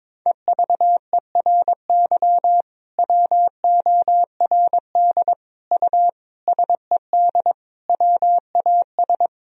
Morse code, heard as a keyed tone, about 22 wpm, 700 Hz. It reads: EVERY WORD U SED WAS
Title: Sentences from Top 400 Words and Abbreviations 22wpm